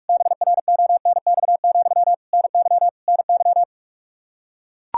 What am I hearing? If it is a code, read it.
Morse code, heard as a keyed tone, about 41 wpm, 700 Hz. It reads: BRONX, NY NY
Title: Practice QSO - Exchange 1 - Beginning - 45wpm